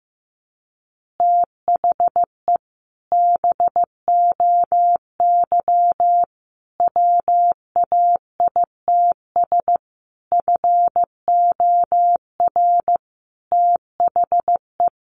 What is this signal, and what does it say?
Morse code, heard as a keyed tone, about 15 wpm, 700 Hz. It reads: THE BOY WAITS FOR THE
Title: Easy Phrases 15wpm